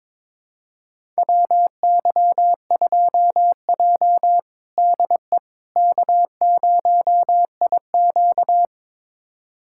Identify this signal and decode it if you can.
Morse code, heard as a keyed tone, about 22 wpm, 700 Hz. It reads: WY2J DE K0IQ
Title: USA Callsign DE Callsign - 4 Characters - 22wpm